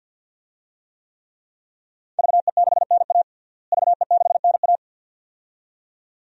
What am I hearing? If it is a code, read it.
Morse code, heard as a keyed tone, about 53 wpm, 700 Hz. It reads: VE6NA VE6NA
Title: DXCC-100 Call Signs - 5 Characters - Repeated 2x - 50wpm